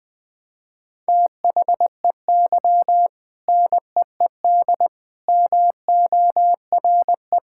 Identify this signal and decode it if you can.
Morse code, heard as a keyed tone, about 21 wpm, 700 Hz. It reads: THEY NEED MORE
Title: Sentences from Top 1000 Words - Review Part 6 - 20wpm